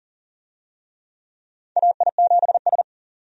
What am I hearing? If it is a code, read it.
Morse code, heard as a keyed tone, about 40 wpm, 700 Hz. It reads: AI7S
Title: USA Call Signs - 4 Characters - 40wpm